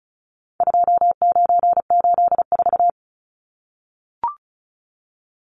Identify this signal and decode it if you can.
Morse code, heard as a keyed tone, about 34 wpm, 700 Hz. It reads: 2984
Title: Numbers - 4-Digit 35wpm